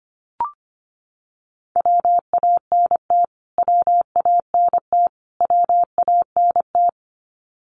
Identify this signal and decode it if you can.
Morse code, heard as a keyed tone, about 25 wpm, 700 Hz. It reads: WANT WANT WANT
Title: Top 200 Words Repeated 25wpm